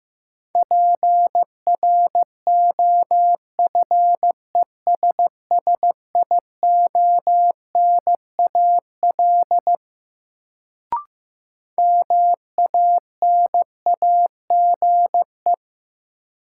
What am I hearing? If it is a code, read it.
Morse code, heard as a keyed tone, about 15 wpm, 700 Hz. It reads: PROFESSIONAL MANAGE
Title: Sentences from Top 1400 Words - New Word Drill Alt - 15wpm